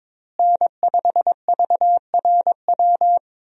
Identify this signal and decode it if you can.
Morse code, heard as a keyed tone, about 22 wpm, 700 Hz. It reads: N5VRW
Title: USA Call Signs - 5 Characters - Repeated 3x - 22wpm